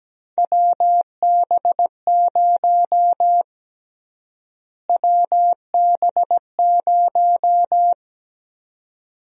Morse code, heard as a keyed tone, about 17 wpm, 700 Hz. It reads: WB0 WB0
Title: USA Call Sign Prefixes - Repeated 2x - 17wpm